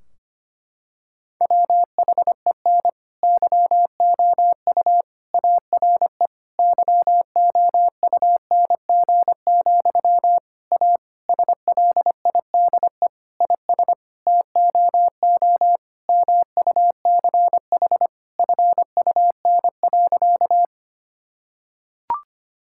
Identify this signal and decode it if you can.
Morse code, heard as a keyed tone, about 25 wpm, 700 Hz. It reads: WHEN YOU ARE YOUNG, A SLIDE IS TOO MUCH FUN.
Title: Sentences from Top 1600 Words 25wpm